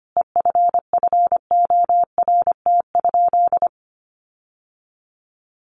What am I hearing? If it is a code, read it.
Morse code, heard as a keyed tone, about 25 wpm, 700 Hz. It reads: EFFORT?
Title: Sentences from Top 1400 Words 25wpm